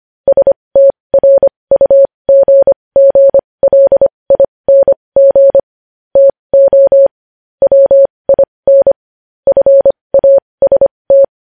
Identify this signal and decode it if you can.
Morse code, heard as a keyed tone, about 25 wpm, spaced as for 20 wpm, 550 Hz. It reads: STRUGGLING TO WIN FAST
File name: podcast / News headlines-1x-25x20